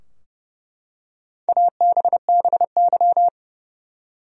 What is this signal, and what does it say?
Morse code, heard as a keyed tone, about 32 wpm, 700 Hz. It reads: ABBY
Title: USA Common Names - Max 4 Letters - 30wpm